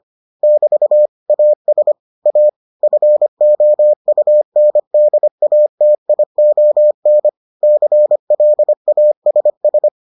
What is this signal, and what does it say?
Morse code, heard as a keyed tone, about 25 wpm, 600 Hz. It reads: <BT> AS A FOUNDATION CLASS